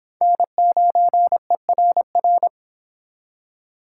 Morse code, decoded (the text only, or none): N9ERR